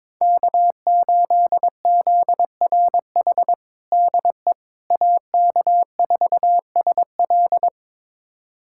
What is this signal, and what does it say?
Morse code, heard as a keyed tone, about 22 wpm, 700 Hz. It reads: K8ZRH DE AK4SL
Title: USA Callsign DE Callsign - 5 Characters - 22wpm